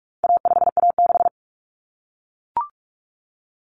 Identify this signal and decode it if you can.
Morse code, heard as a keyed tone, about 44 wpm, 700 Hz. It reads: A5A6
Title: Common Cut Numbers - 4-Digit 45wpm